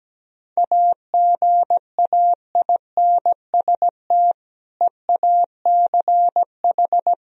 Morse code, heard as a keyed tone, about 17 wpm, 700 Hz. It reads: AGAINST EACH
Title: Sentences from Top 1000 Words - Review Part 1 - 17wpm